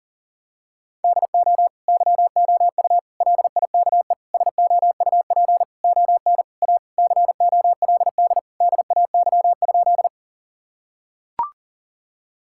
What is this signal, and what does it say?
Morse code, heard as a keyed tone, about 40 wpm, 700 Hz. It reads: DO YOU LIKE SOUP ON A COLD DAY?
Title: Sentences from Top 1000 Words - Review Part 6 - 40wpm